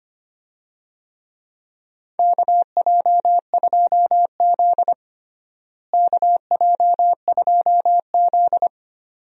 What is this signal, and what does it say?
Morse code, heard as a keyed tone, about 25 wpm, 700 Hz. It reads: KJ2Z KJ2Z
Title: USA Call Signs - 4 Characters - Repeated 2x - 25wpm